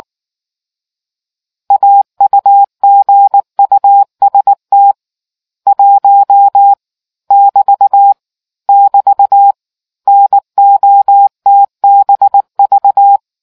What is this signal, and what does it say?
Morse code, heard as a keyed tone, about 19 wpm, 800 Hz. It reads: AUGUST 1 <BT> <BT> NOTBV